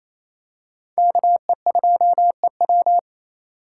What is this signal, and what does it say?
Morse code, heard as a keyed tone, about 28 wpm, 700 Hz. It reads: KE2EW
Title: USA Call Signs - 2x Character Spacing - 28wpm